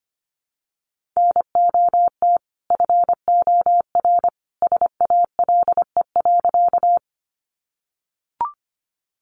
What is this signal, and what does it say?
Morse code, heard as a keyed tone, about 25 wpm, 700 Hz. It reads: NOT FOR SALE.
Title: Sentences from Top 1400 Words 25wpm